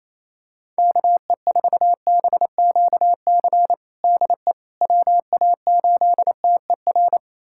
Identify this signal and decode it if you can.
Morse code, heard as a keyed tone, about 28 wpm, 700 Hz. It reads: KE4BQC DE WA8TER
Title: USA Callsign DE Callsign 28wpm